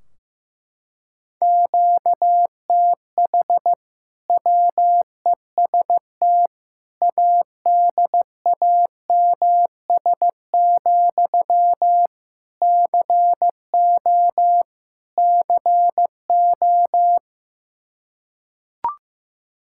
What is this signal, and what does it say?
Morse code, heard as a keyed tone, about 15 wpm, 700 Hz. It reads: QTH WEST ADAMS, CO CO
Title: Practice QSO - Exchange 1 - Beginning - 15wpm